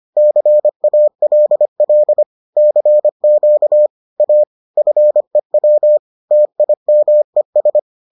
Morse code, heard as a keyed tone, about 25 wpm, 600 Hz. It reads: CALL CQ A FEW TIMES